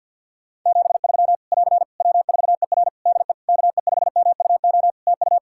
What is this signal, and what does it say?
Morse code, heard as a keyed tone, about 50 wpm, 700 Hz. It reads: 73 <AR> W4ER DE KE5MUO TU
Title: K1USN SST QSO Practice 50wpm